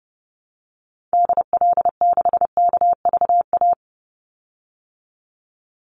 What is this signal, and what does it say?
Morse code, heard as a keyed tone, about 30 wpm, 700 Hz. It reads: DL6KVA
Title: Call Signs Repeated 2x - 30wpm